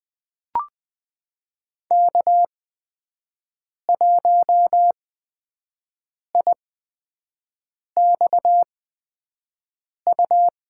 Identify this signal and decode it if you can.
Morse code, heard as a keyed tone, about 20 wpm, spaced as for 5 wpm, 700 Hz. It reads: K1IXU
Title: USA Call Signs - 5 Characters - 8x Character Spacing - 20wpm